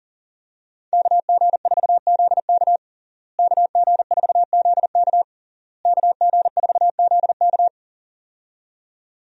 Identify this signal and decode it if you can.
Morse code, heard as a keyed tone, about 40 wpm, 700 Hz. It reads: KG4ZK KG4ZK KG4ZK
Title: USA Call Signs - 5 Characters - Repeated 3x - 40wpm